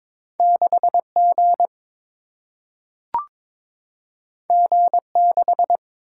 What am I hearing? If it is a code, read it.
Morse code, heard as a keyed tone, about 22 wpm, 700 Hz. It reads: 6 G G 6
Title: Single-Letter Single-Number 22wpm